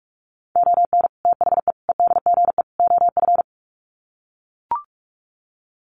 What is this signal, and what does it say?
Morse code, heard as a keyed tone, about 44 wpm, 700 Hz. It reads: ON THE EDGE OF
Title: Sets of 4 Words 45wpm